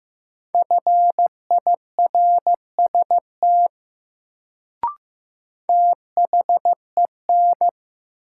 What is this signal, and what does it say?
Morse code, heard as a keyed tone, about 15 wpm, 700 Hz. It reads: FIRST THEN
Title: Sentences from Top 100 Words - New Word Drill - 15wpm